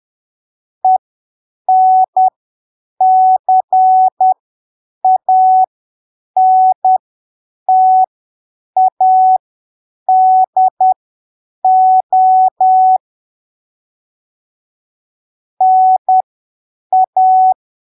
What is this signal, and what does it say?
Morse code, heard as a keyed tone, about 10 wpm, spaced as for 7 wpm, 750 Hz. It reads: ENCANTADO NA
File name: arquivos / texto / texto 29 10ppm